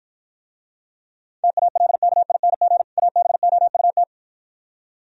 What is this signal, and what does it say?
Morse code, heard as a keyed tone, about 53 wpm, 700 Hz. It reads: TALKING ABOUT